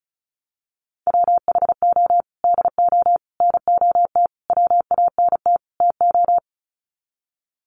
Morse code, heard as a keyed tone, about 34 wpm, 700 Hz. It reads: WHO DO NOT WANT TO
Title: Sets of 5 Words Encore2 35wpm